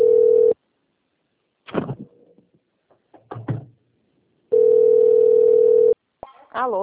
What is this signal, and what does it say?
Ringback tone (North America), 440 + 480 Hz, with speech.
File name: out- - - - - .86993